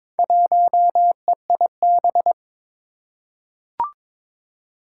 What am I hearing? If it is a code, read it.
Morse code, heard as a keyed tone, about 22 wpm, 700 Hz. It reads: 1EIB
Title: Four Challenging Letter-Number Combinations 22wpm